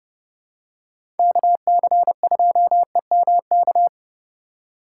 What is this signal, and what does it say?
Morse code, heard as a keyed tone, about 30 wpm, 700 Hz. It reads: KC2EMK
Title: USA Call Signs - 6 Characters - 2x Character Spacing - 30wpm